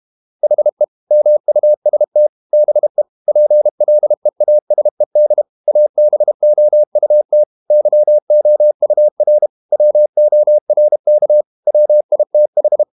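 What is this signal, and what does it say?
Morse code, heard as a keyed tone, about 32 wpm, 600 Hz. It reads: HE MUST BE PLEASED ABOUT YOUR WORK WITH
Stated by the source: Morse code practice at 32 wpm from CW med Gnister, reading Ken Follett's novel Never in English.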